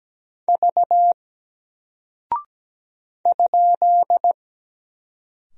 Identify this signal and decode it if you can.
Morse code, heard as a keyed tone, about 17 wpm, 700 Hz. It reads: V ?
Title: Single Character (Letter:Number:Punctuation:Prosign) 17wpm